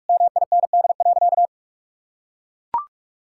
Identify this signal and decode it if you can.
Morse code, heard as a keyed tone, about 41 wpm, 700 Hz. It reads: MIND.
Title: Sentences from Top 1000 Words - Review Part 6 - 45wpm